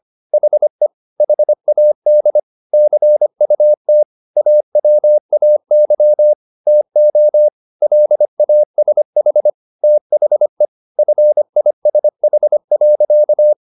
Morse code, heard as a keyed tone, about 25 wpm, 600 Hz. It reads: HE HAD CUT AWAY TO LASH THE FISH.